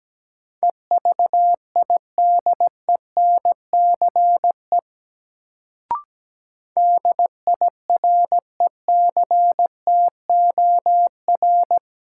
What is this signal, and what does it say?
Morse code, heard as a keyed tone, about 17 wpm, 700 Hz. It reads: EVIDENCE DIRECTOR
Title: Sentences from Top 1300 Words - New Word Drill Alt - 17wpm